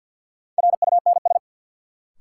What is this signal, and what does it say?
Morse code, heard as a keyed tone, about 53 wpm, 700 Hz. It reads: RUNS
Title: Sentences from Top 600 Words - New Word Drill - 50wpm